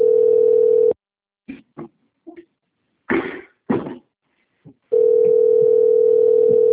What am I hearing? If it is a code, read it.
Ringback tone (North America), 440 + 480 Hz.
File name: out- - - - - .60515